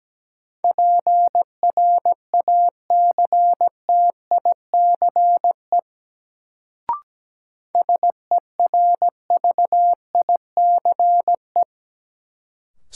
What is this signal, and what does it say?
Morse code, heard as a keyed tone, about 17 wpm, 700 Hz. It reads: PRACTICE SERVICE
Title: Sentences from Top 700 Words - New Word Drill - 17wpm